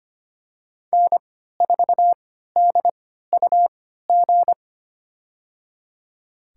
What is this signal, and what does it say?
Morse code, heard as a keyed tone, about 25 wpm, spaced as for 14 wpm, 700 Hz. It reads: N4DUG
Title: USA Call Signs - 5 Characters - 3x Character Spacing - 25wpm